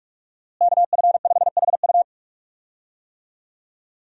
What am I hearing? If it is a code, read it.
Morse code, heard as a keyed tone, about 46 wpm, 700 Hz. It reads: KF5HU
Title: USA Call Signs - 5 Characters - Repeated 2x - 45wpm